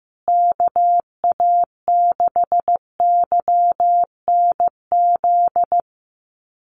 Morse code, heard as a keyed tone, about 15 wpm, 700 Hz. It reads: KA6YNZ
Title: USA Call Signs Repeated 2x 15wpm